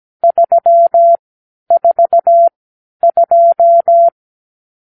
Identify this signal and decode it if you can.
Morse code, heard as a keyed tone, about 17 wpm, spaced as for 10 wpm, 680 Hz. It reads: 342
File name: k7qo / Lesson 52 (8)